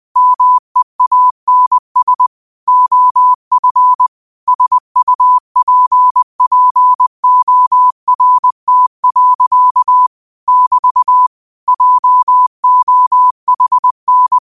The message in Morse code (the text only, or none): MEANS OF SUPPORT. <BT> JOHN